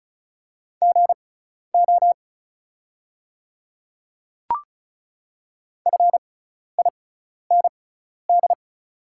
Morse code, heard as a keyed tone, about 36 wpm, spaced as for 12 wpm, 690 Hz. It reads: GO FIND
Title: 6x Character Spacing - Top 100 Words - 35wpm